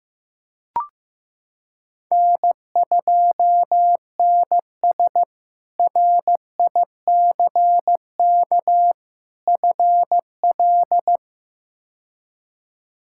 Morse code, heard as a keyed tone, about 15 wpm, 700 Hz. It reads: N2NS RICK FL
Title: K1USN SST QSO Practice - 15wpm